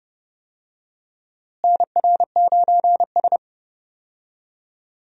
Morse code, heard as a keyed tone, about 32 wpm, 700 Hz. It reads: NR9S
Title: USA Call Signs - 4 Characters - Repeated 2x - 30wpm